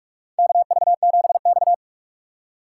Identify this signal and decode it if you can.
Morse code, heard as a keyed tone, about 44 wpm, 690 Hz. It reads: KV7X
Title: USA Call Signs - 4 Characters - 4x Character Spacing - 45wpm